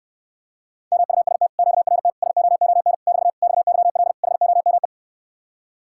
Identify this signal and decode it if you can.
Morse code, heard as a keyed tone, about 69 wpm, 700 Hz. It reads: MUST QRT SOON <BT> XYL HOME